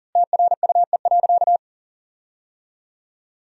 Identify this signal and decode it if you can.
Morse code, heard as a keyed tone, about 40 wpm, 700 Hz. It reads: TRUE.
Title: 3x Word-Spacing - Sentences from Top 400 Words - 40wpm